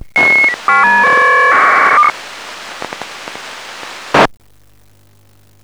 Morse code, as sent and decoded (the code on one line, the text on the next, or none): none